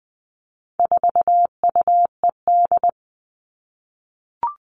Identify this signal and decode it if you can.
Morse code, heard as a keyed tone, about 20 wpm, 700 Hz. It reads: 4UED
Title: All Cut Numbers - 4-Digit 20wpm